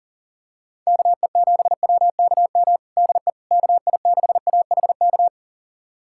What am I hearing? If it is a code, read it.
Morse code, heard as a keyed tone, about 40 wpm, 700 Hz. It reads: KE7WKM DE KI6AHK
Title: USA Callsign DE Callsign - 6 Characters - 40wpm